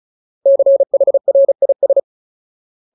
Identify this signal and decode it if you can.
Morse code, heard as a keyed tone, about 34 wpm, 550 Hz. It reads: CHRIS
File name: Ultimate Word List / 35WPM Ultimate